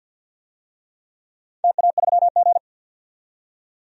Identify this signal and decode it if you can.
Morse code, heard as a keyed tone, about 47 wpm, 700 Hz. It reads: TA3G
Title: DXCC-100 Call Signs - 4 Characters - 50wpm